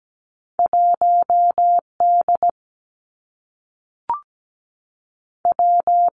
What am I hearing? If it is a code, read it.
Morse code, heard as a keyed tone, about 17 wpm, 700 Hz. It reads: 1 D W
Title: Two Challenging Letter-Number Combinations 17wpm